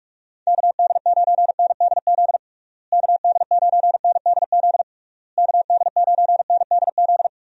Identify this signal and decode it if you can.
Morse code, heard as a keyed tone, about 46 wpm, 700 Hz. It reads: KD9NDZ KD9NDZ KD9NDZ
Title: USA Call Signs - 6 Characters - Repeated 3x - 45wpm